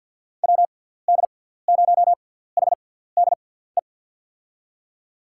Morse code, heard as a keyed tone, about 46 wpm, spaced as for 17 wpm, 700 Hz. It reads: WD0HDE
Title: USA Call Signs - 6 Characters - 6x Character Spacing - 50wpm